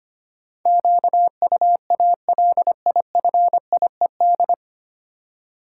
Morse code, heard as a keyed tone, about 25 wpm, 700 Hz. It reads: QUALIFIED
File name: practice / 25wpm / 9 Letter Words - First 3 Letters Given - 25wpm